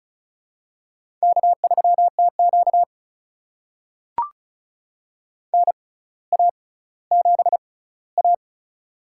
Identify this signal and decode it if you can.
Morse code, heard as a keyed tone, about 36 wpm, spaced as for 18 wpm, 700 Hz. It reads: K3TQ NA7A
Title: USA Call Signs - 4 Characters - 6x Character Spacing - 35wpm